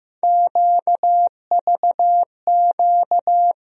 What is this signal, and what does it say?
Morse code, heard as a keyed tone, about 15 wpm, 700 Hz. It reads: QVQ
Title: USA Call Sign Postfixes - 3 Letters 15wpm